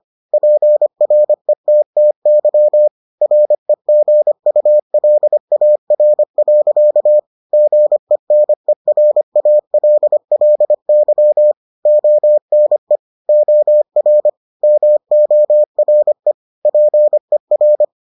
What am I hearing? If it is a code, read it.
Morse code, heard as a keyed tone, about 25 wpm, 600 Hz. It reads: PRETTY REGULAR. GENERALLY ONE OR MORE PER